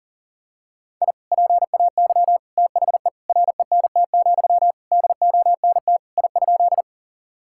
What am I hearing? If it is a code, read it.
Morse code, heard as a keyed tone, about 40 wpm, 700 Hz. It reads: I PAY THE RENT, DONT I?
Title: Sentences from Top 1000 Words - Review Part 5 - 40wpm